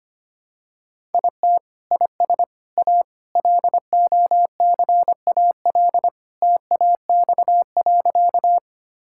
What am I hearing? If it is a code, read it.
Morse code, heard as a keyed tone, about 25 wpm, 700 Hz. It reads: IT IS A LOCAL TAX.
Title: Sentences from Top 1000 Words - Review Part 5 - 25wpm